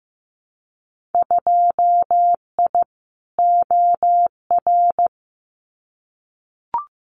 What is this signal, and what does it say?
Morse code, heard as a keyed tone, about 15 wpm, 700 Hz. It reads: 2I OR
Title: Winter Field Day Exchanges Encore 15wpm